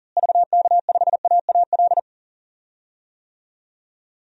Morse code, heard as a keyed tone, about 40 wpm, 700 Hz. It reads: VK5AAL
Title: DXCC-100 Call Signs - 6 Characters - Repeated 2x - 40wpm